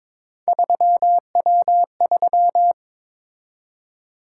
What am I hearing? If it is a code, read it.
Morse code, heard as a keyed tone, about 22 wpm, 700 Hz. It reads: 3W3
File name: practice / 22wpm / Three Letter-Number Combinations - 22wpm